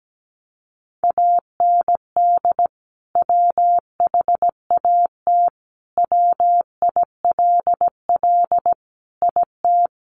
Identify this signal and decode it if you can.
Morse code, heard as a keyed tone, about 17 wpm, 700 Hz. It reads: AND WHAT WILL IT
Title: Sentences from Top 1200 Words 17wpm